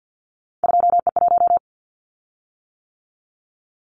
Morse code, heard as a keyed tone, about 50 wpm, 700 Hz. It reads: 2E1
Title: Callsign Prefixes for Top 25 DXCC - 50wpm